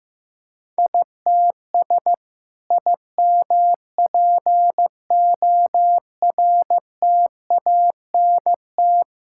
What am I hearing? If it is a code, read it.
Morse code, heard as a keyed tone, about 14 wpm, 700 Hz. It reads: ITS IMPORTANT
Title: Sentences from Top 1000 Words - Review Part 5 - 15wpm